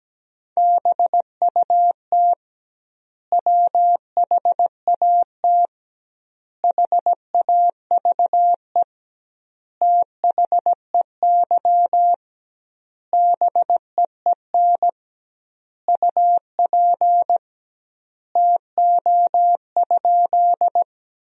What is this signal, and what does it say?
Morse code, heard as a keyed tone, about 17 wpm, 700 Hz. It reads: BUT WHAT HAVE THEY BEEN UP TO?
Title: 2x Word-Spacing - Sentences from Top 100 Words - 17wpm